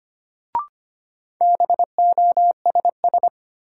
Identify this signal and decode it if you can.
Morse code, heard as a keyed tone, about 25 wpm, 700 Hz. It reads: BOSS
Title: Sentences from Top 700 Words - New Word Drill - 25wpm